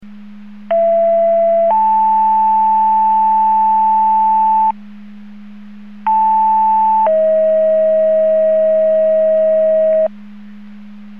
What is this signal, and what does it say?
Two-tone page, heard as a one-second tone, then a three-second tone, about 670 then 880 Hz, about 880 then 640 Hz. It is heard from 0.7 to 10.1 s.
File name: Portage 154.385 Paging